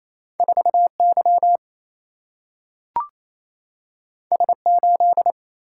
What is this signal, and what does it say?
Morse code, heard as a keyed tone, about 28 wpm, 700 Hz. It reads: 4Y S8
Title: Single-Letter Single-Number 28wpm